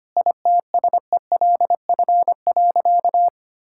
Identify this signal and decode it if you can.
Morse code, heard as a keyed tone, about 25 wpm, 700 Hz. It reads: ITSELF.
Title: Sentences from Top 1000 Words - Review Part 5 - 25wpm